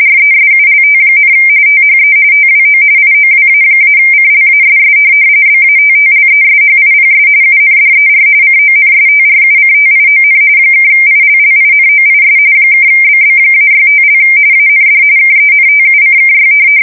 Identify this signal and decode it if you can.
RTTY, about 45.45 baud. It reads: RESULT, THE TWO IOWA-CLASS BATTLESHIPS UNDER CONSTRUCTION (ILLINOIS (BB-65) AND KENTUCKY
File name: fsk170 / USS-New-Jersey